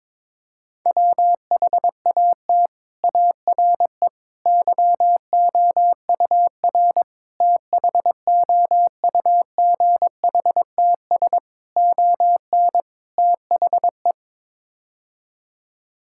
Morse code, heard as a keyed tone, about 21 wpm, 700 Hz. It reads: WHAT ARE YOUR THOUGHTS ON THE
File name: practice / 22wpm / Sets of 6 Words - 22wpm